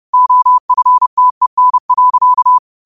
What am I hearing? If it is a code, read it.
Morse code, heard as a keyed tone, about 30 wpm, 1000 Hz. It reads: OFTEN.
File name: qotdfiles / 30wpm